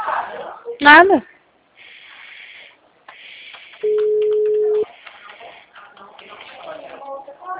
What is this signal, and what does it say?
Ringback tone (Europe, 425 Hz), with speech.